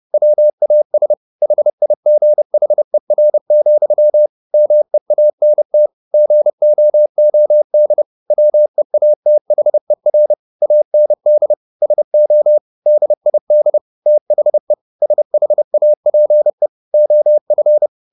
Morse code, heard as a keyed tone, about 30 wpm, 600 Hz. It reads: WAS HIGHER, MEANT GOOD WEATHER AND SO DID THE SHAPE OF
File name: the old man and the sea / the old man and the sea 1 30wpm